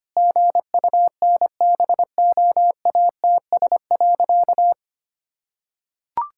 Morse code, heard as a keyed tone, about 25 wpm, 700 Hz. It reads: GUNBOATS.
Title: Pangram Challenge v2 - 25wpm